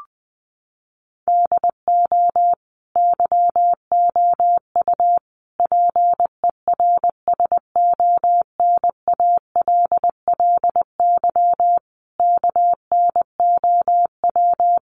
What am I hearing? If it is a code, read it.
Morse code, heard as a keyed tone, about 20 wpm, 700 Hz. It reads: DO YOU PERSONALLY KNOW
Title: Sentences from Top 1400 Words 20wpm